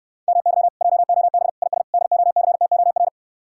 Morse code, heard as a keyed tone, about 69 wpm, 700 Hz. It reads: MY JOB IS DOCTOR